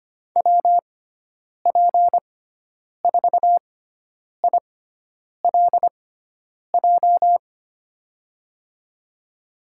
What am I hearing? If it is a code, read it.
Morse code, heard as a keyed tone, about 25 wpm, spaced as for 9 wpm, 700 Hz. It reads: WP4ILJ
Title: USA Call Signs - 6 Characters - 6x Character Spacing - 25wpm